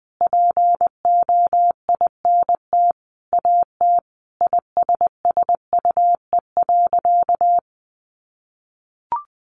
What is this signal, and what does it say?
Morse code, heard as a keyed tone, about 20 wpm, 700 Hz. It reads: POINT AT ISSUE.
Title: Sentences from Top 900 Words 20wpm